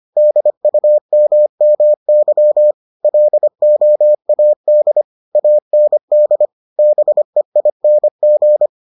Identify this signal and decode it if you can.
Morse code, heard as a keyed tone, about 25 wpm, 600 Hz. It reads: DUMMY LOAD AND BEING